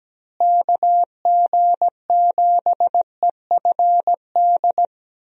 Morse code, heard as a keyed tone, about 17 wpm, 700 Hz. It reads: KG7EFD
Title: USA Call Signs - 6 Characters - Repeated 3x - 17wpm